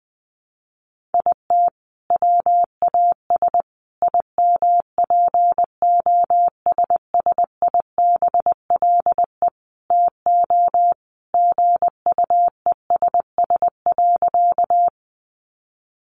Morse code, heard as a keyed tone, about 19 wpm, 700 Hz. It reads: IT WAS IMPOSSIBLE TO GUESS.
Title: Sentences from Top 1500 Words 20wpm